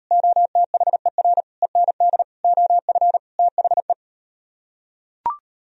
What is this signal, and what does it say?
Morse code, heard as a keyed tone, about 38 wpm, 700 Hz. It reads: OTHER END OF THE